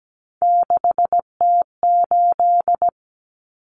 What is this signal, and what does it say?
Morse code, heard as a keyed tone, about 17 wpm, 700 Hz. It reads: 6T8
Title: Common Cut Numbers - 3-Digit 17wpm